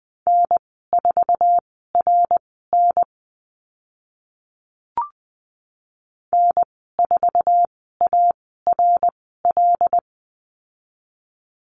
Morse code, heard as a keyed tone, about 20 wpm, spaced as for 14 wpm, 700 Hz. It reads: N4RN N4ARL
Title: 082 Lesson 14g - Number 2 - Call Signs 20wpm